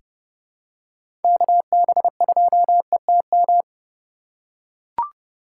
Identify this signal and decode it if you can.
Morse code, heard as a keyed tone, about 30 wpm, 700 Hz. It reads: KB2ETM
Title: USA Call Signs - 6 Characters - 2x Character Spacing - 30wpm